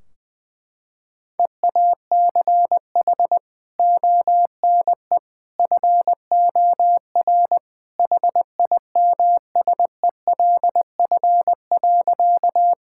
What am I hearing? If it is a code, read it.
Morse code, heard as a keyed tone, about 21 wpm, 700 Hz. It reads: EACH ONE FOR HIMSELF.
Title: Sentences from Top 1600 Words 20wpm